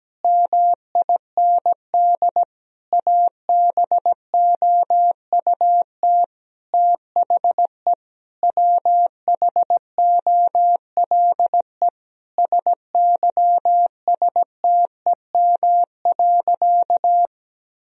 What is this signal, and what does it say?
Morse code, heard as a keyed tone, about 17 wpm, 700 Hz. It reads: MIND ABOUT THE WHOLE SYSTEM.
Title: Sentences from Top 1000 Words - Review Part 3 - 17wpm